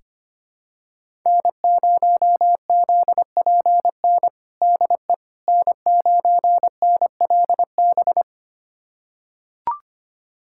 Morse code, heard as a keyed tone, about 25 wpm, 700 Hz. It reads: N0ZPN DE N9NLB
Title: USA Callsign DE Callsign 25wpm